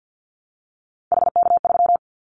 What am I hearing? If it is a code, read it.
Morse code, heard as a keyed tone, about 50 wpm, 700 Hz. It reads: HK3
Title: Callsign Prefixes for Top 100 DXCC - 50wpm